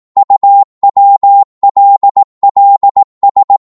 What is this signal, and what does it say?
Morse code, heard as a keyed tone, about 18 wpm, 800 Hz. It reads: UWLLS